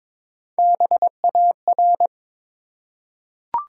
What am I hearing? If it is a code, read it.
Morse code, heard as a keyed tone, about 22 wpm, 700 Hz. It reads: BAR
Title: Three Letter Words - 22wpm